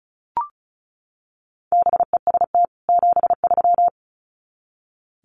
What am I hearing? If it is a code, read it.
Morse code, heard as a keyed tone, about 34 wpm, 700 Hz. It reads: BEST 73
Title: Common QSO Phrases 35wpm